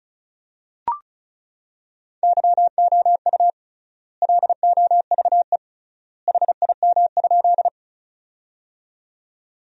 Morse code, heard as a keyed tone, about 36 wpm, 700 Hz. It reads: YOU LOVE HIM?
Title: 3x Word-Spacing - Sentences from Top 400 Words - 35wpm